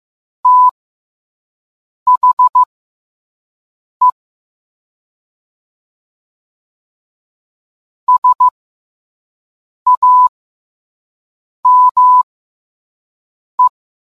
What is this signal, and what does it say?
Morse code, heard as a keyed tone, about 15 wpm, spaced as for 5 wpm, 1000 Hz. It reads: THE SAME